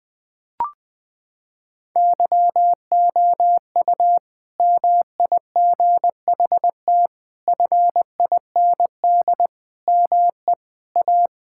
Morse code, heard as a keyed tone, about 21 wpm, 700 Hz. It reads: YOU MIGHT FIND ME A
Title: Sentences from Top 1600 Words 20wpm